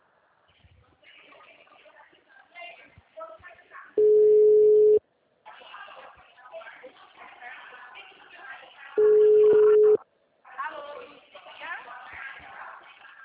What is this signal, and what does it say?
Ringback tone (Europe, 425 Hz), with speech.
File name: out- -2049- - - .74994